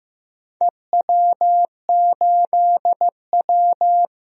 Morse code, heard as a keyed tone, about 15 wpm, 700 Hz. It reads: EW8W
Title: DXCC-100 Call Signs - 4 Characters - 15wpm